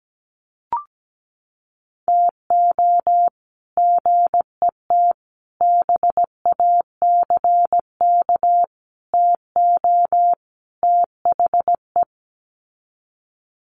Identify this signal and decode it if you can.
Morse code, heard as a keyed tone, about 17 wpm, 700 Hz. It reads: TO GET BACK TO THE
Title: Sets of 5 Words Encore 17wpm